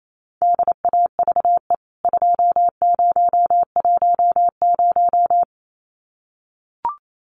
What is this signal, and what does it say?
Morse code, heard as a keyed tone, about 28 wpm, 700 Hz. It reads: DAVE 2010
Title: CWT Exchange 28wpm